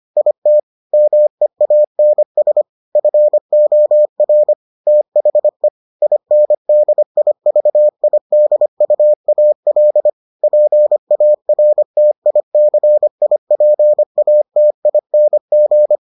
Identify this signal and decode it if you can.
Morse code, heard as a keyed tone, about 25 wpm, 600 Hz. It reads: IT MEANS FOR THE INDIVIDUAL PARTICIPATING